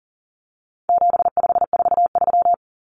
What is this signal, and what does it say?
Morse code, heard as a keyed tone, about 40 wpm, 700 Hz. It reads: 7543
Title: Common Cut Numbers - 4-Digit 40wpm